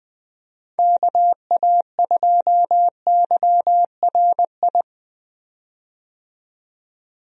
Morse code, heard as a keyed tone, about 20 wpm, 700 Hz. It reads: KA2YRI
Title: USA Call Signs - 6 Characters - 20wpm